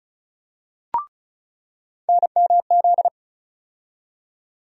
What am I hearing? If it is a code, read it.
Morse code, heard as a keyed tone, about 34 wpm, 700 Hz. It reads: NMZ
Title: USA Call Sign Postfixes - 3 Letters 35wpm